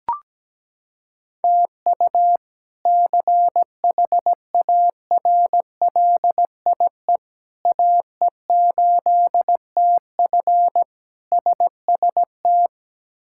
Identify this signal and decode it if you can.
Morse code, heard as a keyed tone, about 17 wpm, 700 Hz. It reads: TU CHARLIE AE8TF SST
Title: K1USN SST QSO Practice - 17wpm